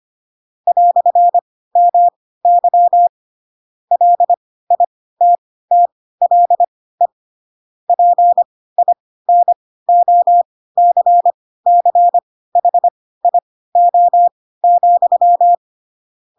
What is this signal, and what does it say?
Morse code, heard as a keyed tone, about 25 wpm, spaced as for 16 wpm, 700 Hz. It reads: "MY LITTLE PINOCCHIO,
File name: Pinocchio / Pinocchio ch7 10 25 ik1hge